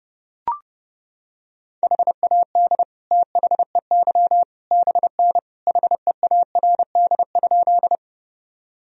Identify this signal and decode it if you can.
Morse code, heard as a keyed tone, about 30 wpm, 700 Hz. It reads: HAD THEY BN HEARD?
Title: Sentences from Top 400 Words and Abbreviations 30wpm